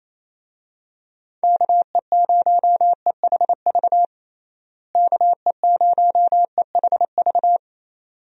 Morse code, heard as a keyed tone, about 28 wpm, 700 Hz. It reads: KE0EHV KE0EHV
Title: USA Call Signs - 6 Characters - Repeated 2x - 28wpm